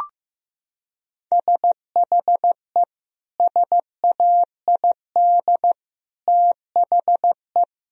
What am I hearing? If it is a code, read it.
Morse code, heard as a keyed tone, about 15 wpm, 700 Hz. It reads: SHE SAID THE
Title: Sentences from Top 1000 Words - Review Part 1 - 15wpm